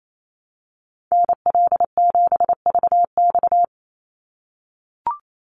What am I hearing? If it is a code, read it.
Morse code, heard as a keyed tone, about 28 wpm, 700 Hz. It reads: NL7VX
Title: Call Signs - 28wpm